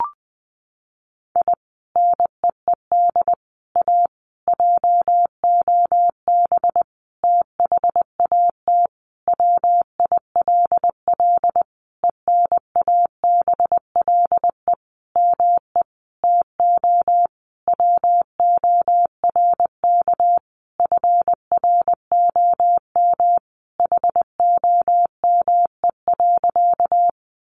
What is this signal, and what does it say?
Morse code, heard as a keyed tone, about 20 wpm, 700 Hz. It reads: I NEED A JOB THAT WILL ENABLE ME TO WORK FROM HOME.
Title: Sentences from Top 1500 Words 20wpm